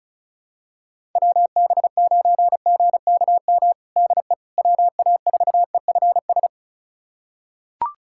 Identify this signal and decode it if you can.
Morse code, heard as a keyed tone, about 34 wpm, 700 Hz. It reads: WB9GKM DE WA4EFS
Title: USA Callsign DE Callsign - 6 Characters - 35wpm